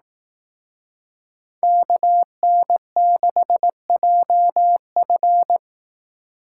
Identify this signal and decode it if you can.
Morse code, heard as a keyed tone, about 18 wpm, 700 Hz. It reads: KN6JF
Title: Speed Racing - USA 2x2 Call Signs 15wpm